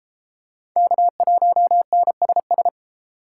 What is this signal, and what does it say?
Morse code, heard as a keyed tone, about 32 wpm, 700 Hz. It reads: K1NSS